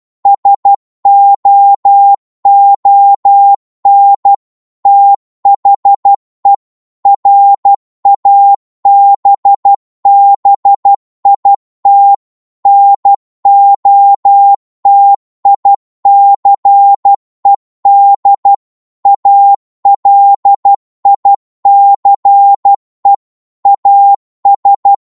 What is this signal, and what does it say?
Morse code, heard as a keyed tone, about 12 wpm, 800 Hz. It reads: SOON THE RABBIT NOTICED ALICE AS